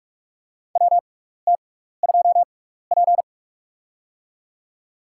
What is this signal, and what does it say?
Morse code, heard as a keyed tone, about 46 wpm, spaced as for 15 wpm, 700 Hz. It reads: WT2P
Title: USA Call Signs - 4 Characters - 6x Character Spacing - 45wpm